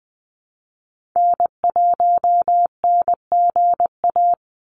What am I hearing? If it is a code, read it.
Morse code, heard as a keyed tone, about 20 wpm, 700 Hz. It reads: N1NGA
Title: USA Call Signs 20wpm